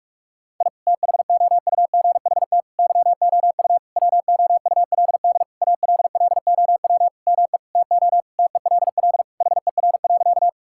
Morse code, heard as a keyed tone, about 46 wpm, 700 Hz. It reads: I THOUGHT YOU WOULD ALLOW ME TO TELL HER.
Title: Sentences from Top 1000 Words - Review Part 5 - 45wpm